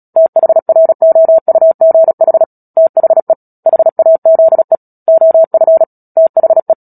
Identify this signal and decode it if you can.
Morse code, heard as a keyed tone, about 37 wpm, 650 Hz. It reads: THROUGH THE HAZE OF THE
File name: Book Two / 10-Chapter 10 - Epilogue - 36 WPM